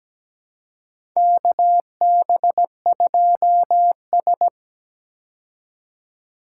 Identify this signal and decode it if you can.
Morse code, heard as a keyed tone, about 17 wpm, 700 Hz. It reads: KB2S
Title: K1USN SST QSO Practice - 17wpm